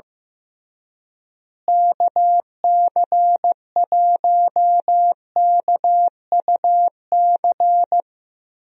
Morse code, heard as a keyed tone, about 15 wpm, 700 Hz. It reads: KC1KUC
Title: USA Call Signs - 6 Characters - Repeated 3x - 15wpm